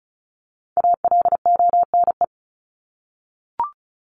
Morse code, heard as a keyed tone, about 34 wpm, 700 Hz. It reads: ALONE
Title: Top 751-1000 Words 35wpm